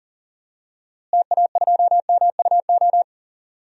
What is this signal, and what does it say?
Morse code, heard as a keyed tone, about 40 wpm, 700 Hz. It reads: TA2MUO
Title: DXCC-100 Call Signs - 6 Characters - Repeated 3x - 40wpm